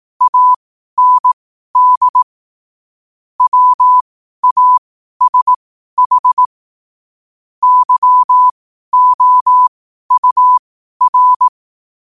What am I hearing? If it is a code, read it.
Morse code, heard as a keyed tone, about 18 wpm, spaced as for 12 wpm, 1000 Hz. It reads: AND WASH YOUR
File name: qotdfiles / 12wpm